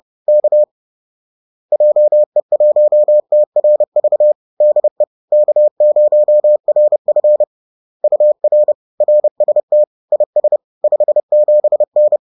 Morse code, heard as a keyed tone, about 30 wpm, 600 Hz. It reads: K JE1TRV DE K0RF UR RST IS 57N